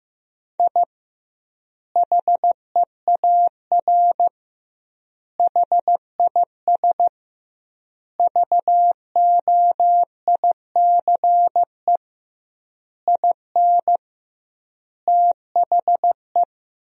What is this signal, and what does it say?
Morse code, heard as a keyed tone, about 15 wpm, 700 Hz. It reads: I HEAR HIS VOICE IN THE
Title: 2x Word-Spacing - Sentences from Top 500 Words - 15wpm